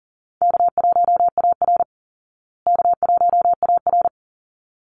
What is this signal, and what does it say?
Morse code, heard as a keyed tone, about 40 wpm, 700 Hz. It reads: K1AR K1AR
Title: Easy Call Signs Repeated 2x - 40wpm